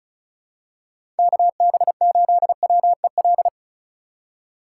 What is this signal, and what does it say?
Morse code, heard as a keyed tone, about 36 wpm, 700 Hz. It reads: KB8WEL
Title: USA Call Signs - 6 Characters - 3x Character Spacing - 35wpm